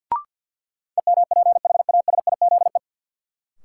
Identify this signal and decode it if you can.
Morse code, heard as a keyed tone, about 51 wpm, 700 Hz. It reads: EMPHASIZE
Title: Sentences from Top 1500 Words - New Word Drill Alt - 50wpm